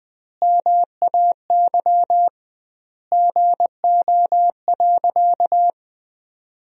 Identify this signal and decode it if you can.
Morse code, heard as a keyed tone, about 20 wpm, 700 Hz. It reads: MAY GO.
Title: 2x Word-Spacing - Sentences from Top 100 Words - 20wpm